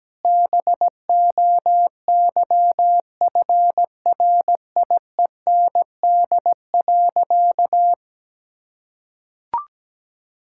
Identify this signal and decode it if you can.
Morse code, heard as a keyed tone, about 17 wpm, 700 Hz. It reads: BOYFRIEND.
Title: Sentences from Top 1000 Words - Review Part 5 - 17wpm